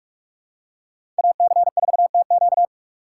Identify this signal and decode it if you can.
Morse code, heard as a keyed tone, about 48 wpm, 690 Hz. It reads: AC4TQ
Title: USA Call Signs - 4x Character Spacing - 45wpm